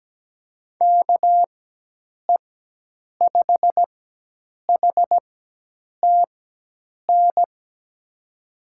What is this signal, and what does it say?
Morse code, heard as a keyed tone, about 17 wpm, spaced as for 8 wpm, 700 Hz. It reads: KE5HTN
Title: USA Call Signs - 6 Characters - 4x Character Spacing - 17wpm